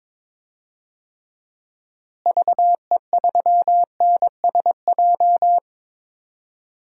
Morse code, heard as a keyed tone, about 22 wpm, 700 Hz. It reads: VE3NSJ
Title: DXCC-100 Call Signs - 6 Characters - Repeated 3x - 22wpm